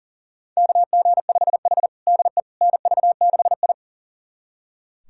Morse code, heard as a keyed tone, about 37 wpm, 700 Hz. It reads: KG5H DE NV6I
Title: USA Callsign DE Callsign - 4 Characters - 40wpm